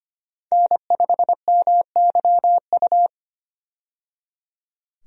Morse code, heard as a keyed tone, about 25 wpm, 700 Hz. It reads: N5MYU
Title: USA Call Signs - 5 Characters - 25wpm